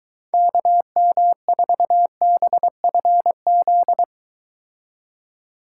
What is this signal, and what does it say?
Morse code, heard as a keyed tone, about 23 wpm, 700 Hz. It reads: KM4BFZ